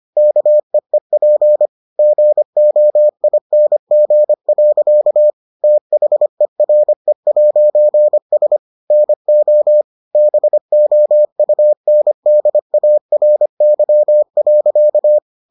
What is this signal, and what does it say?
Morse code, heard as a keyed tone, about 25 wpm, 600 Hz. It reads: KEEP GOING. THERE'S NO BOUNDARY.